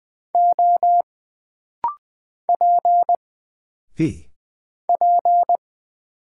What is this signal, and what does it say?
Morse code, heard as a keyed tone, about 20 wpm, 700 Hz. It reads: O P P
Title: Single Character (Letter:Number:Punctuation:Prosign) - Rapid-Fire - 20wpm